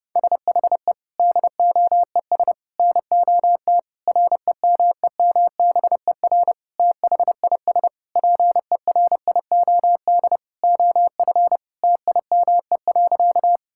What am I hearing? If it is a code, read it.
Morse code, heard as a keyed tone, about 30 wpm, 700 Hz. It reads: SHE DOES NOT REMEMBER THIS PERIOD OF TIME.
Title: Sentences from Top 1000 Words - Review Part 5 - 30wpm